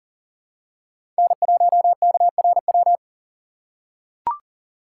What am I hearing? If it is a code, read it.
Morse code, heard as a keyed tone, about 40 wpm, 700 Hz. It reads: N1KRW
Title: USA Call Signs - 5 Characters - 3x Character Spacing - 40wpm